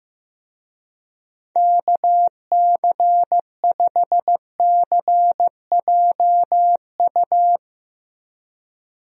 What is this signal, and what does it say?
Morse code, heard as a keyed tone, about 15 wpm, 700 Hz. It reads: KC5CJU
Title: USA Call Signs - 6 Characters - Repeated 3x - 15wpm